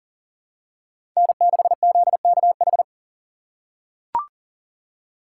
Morse code, heard as a keyed tone, about 40 wpm, 700 Hz. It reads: N6ZKH
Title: USA Call Signs - 5 Characters - 2x Character Spacing - 40wpm